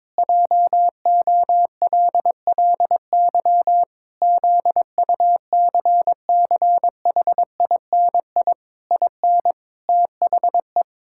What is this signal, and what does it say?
Morse code, heard as a keyed tone, about 22 wpm, 700 Hz. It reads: JOLLY ZUCCHINI IN THE
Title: Pangram Challenge v2 - 22wpm